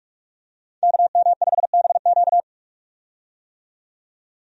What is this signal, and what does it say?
Morse code, heard as a keyed tone, about 46 wpm, 700 Hz. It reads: KM5BQ
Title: USA Call Signs - 5 Characters - 45wpm